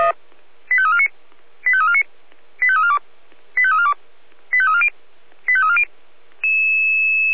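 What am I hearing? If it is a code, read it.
Five-tone selective call (ZVEI), heard as quick runs of five tones, tones of about 70 ms. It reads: 85349 85349 85342 85342 85340 85340
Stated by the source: ZVEI five-tone selective call received by SDR.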